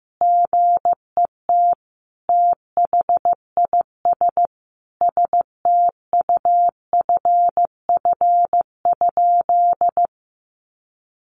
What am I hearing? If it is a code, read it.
Morse code, heard as a keyed tone, about 15 wpm, 700 Hz. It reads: GET THIS STUFF?
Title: Sentences from Top 1400 Words 15wpm